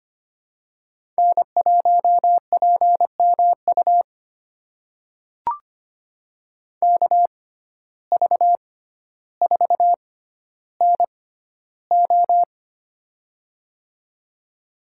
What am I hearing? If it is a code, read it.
Morse code, heard as a keyed tone, about 25 wpm, spaced as for 13 wpm, 700 Hz. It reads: N1PMU KV4NO
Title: USA Call Signs - 5 Characters - 6x Character Spacing - 25wpm